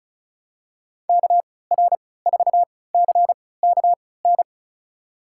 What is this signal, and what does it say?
Morse code, heard as a keyed tone, about 36 wpm, spaced as for 20 wpm, 700 Hz. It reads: KR4CKN
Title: USA Call Signs - 3x Character Spacing - 35wpm